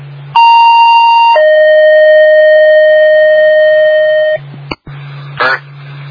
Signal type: two-tone page